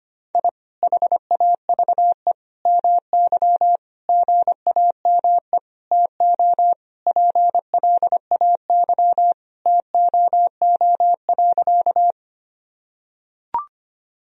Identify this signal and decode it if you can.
Morse code, heard as a keyed tone, about 25 wpm, 700 Hz. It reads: I HAVE MY GAME TO PLAY TOO.
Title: Sentences from Top 1000 Words - Review Part 3 - 25wpm